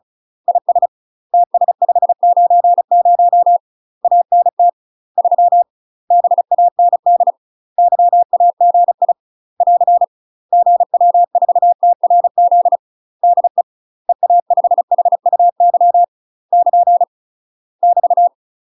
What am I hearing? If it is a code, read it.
Morse code, heard as a keyed tone, about 35 wpm, 700 Hz. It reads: IS TS590 ANT 3 BAND YAGI <AR> GW4TRZ DE EA5HUY <KN> <BT>
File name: UK MASTER QSOs TRAINING / 35WPM / UK MASTER QSOs 0003